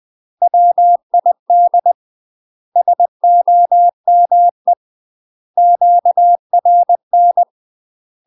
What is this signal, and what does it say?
Morse code, heard as a keyed tone, about 20 wpm, 700 Hz. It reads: WID SOME QRN
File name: UK MASTER QSOs TRAINING / 20WPM / UK MASTER QSOs 0003